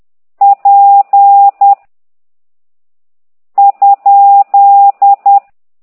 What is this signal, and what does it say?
Morse code, heard as a keyed tone, about 10 wpm, 800 Hz. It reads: P ?
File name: ct2idw / exerc13